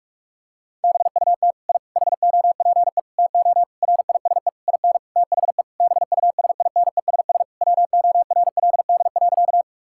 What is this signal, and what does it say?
Morse code, heard as a keyed tone, about 46 wpm, 700 Hz. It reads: BUT I HOPE TO RISE IN THE BUSINESS WORLD.
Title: Sentences from Top 1000 Words - Review Part 6 - 45wpm